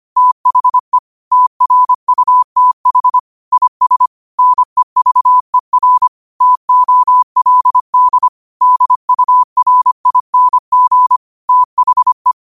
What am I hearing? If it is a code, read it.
Morse code, heard as a keyed tone, about 25 wpm, 1000 Hz. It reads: THE TRUTH IS NEVER TOLD DURING THE
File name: qotdfiles / 25wpm